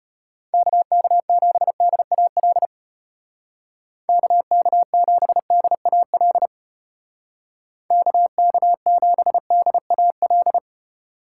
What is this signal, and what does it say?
Morse code, heard as a keyed tone, about 33 wpm, 700 Hz. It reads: KK7DAL KK7DAL KK7DAL